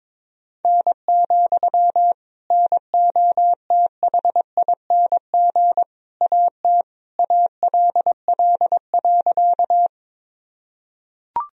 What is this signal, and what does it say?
Morse code, heard as a keyed tone, about 22 wpm, 700 Hz. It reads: N, NOTHING AT ALL.
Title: Sentences from Top 500 Words and Abbreviations 22wpm